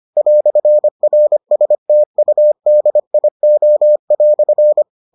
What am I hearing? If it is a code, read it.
Morse code, heard as a keyed tone, about 25 wpm, 600 Hz. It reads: "RSTUDIO"